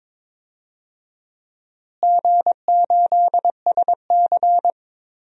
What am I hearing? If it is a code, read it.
Morse code, heard as a keyed tone, about 22 wpm, 700 Hz. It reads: G8SC
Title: DXCC-100 Call Signs - 4 Characters - Repeated 3x - 22wpm